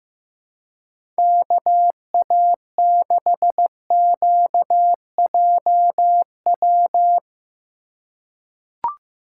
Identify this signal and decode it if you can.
Morse code, heard as a keyed tone, about 15 wpm, 700 Hz. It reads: KA6QJW
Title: USA Call Signs - 2x Character Spacing - 15wpm